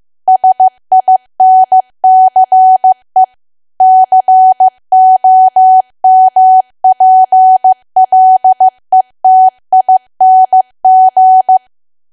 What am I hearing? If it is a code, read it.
Morse code, heard as a keyed tone, about 15 wpm, 750 Hz. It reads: SINCE COMPLETING
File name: cwa / W1AW-Level2 / 15WPM